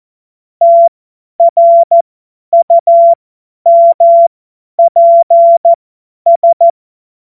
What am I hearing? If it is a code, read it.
Morse code, heard as a keyed tone, about 14 wpm, spaced as for 10 wpm, 680 Hz. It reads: TRUMPS
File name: Morsekurs / Witze / Witz 09. November 2021, 50 F70